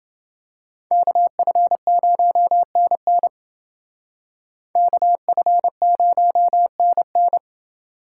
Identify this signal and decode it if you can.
Morse code, heard as a keyed tone, about 28 wpm, 700 Hz. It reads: KF0NN KF0NN